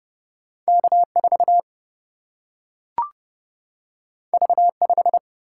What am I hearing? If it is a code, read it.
Morse code, heard as a keyed tone, about 30 wpm, 700 Hz. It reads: K4 V5
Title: Single-Letter Single-Number 30wpm